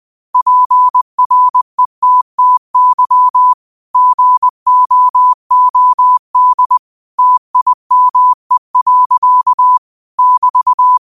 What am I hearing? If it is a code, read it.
Morse code, heard as a keyed tone, about 20 wpm, 1000 Hz. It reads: PRETTY GOOD TIME. <BT>